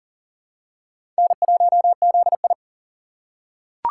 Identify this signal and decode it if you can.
Morse code, heard as a keyed tone, about 40 wpm, 700 Hz. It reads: N1ZI
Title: USA Call Signs - 6x Character Spacing - 40wpm